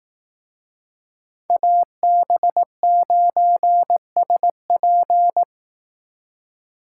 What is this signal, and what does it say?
Morse code, heard as a keyed tone, about 18 wpm, 700 Hz. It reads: AB9SP
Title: Speed Racing - USA 2x2 Call Signs 15wpm